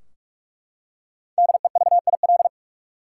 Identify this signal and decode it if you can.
Morse code, heard as a keyed tone, about 44 wpm, 700 Hz. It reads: DEVIL
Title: Five Letter Words - 45wpm